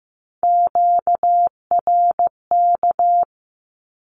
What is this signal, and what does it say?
Morse code, heard as a keyed tone, about 15 wpm, 700 Hz. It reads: QRK
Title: Q-Codes 15wpm